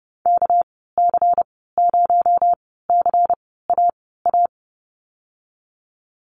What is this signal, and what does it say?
Morse code, heard as a keyed tone, about 30 wpm, spaced as for 17 wpm, 700 Hz. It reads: KC0CAA
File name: Lesson 37g - Letter x - Call Signs 30wpm